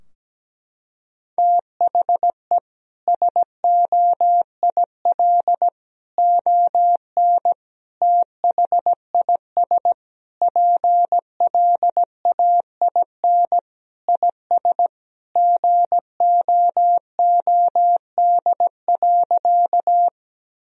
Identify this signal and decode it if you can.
Morse code, heard as a keyed tone, about 17 wpm, 700 Hz. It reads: THE SOIL ON THIS PLAIN IS GOOD.
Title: Sentences from Top 1000 Words - Review Part 4 - 17wpm